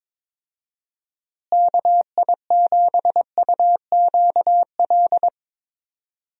Morse code, heard as a keyed tone, about 22 wpm, 700 Hz. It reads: KI7UQL
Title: USA Call Signs - 6 Characters - Repeated 2x - 22wpm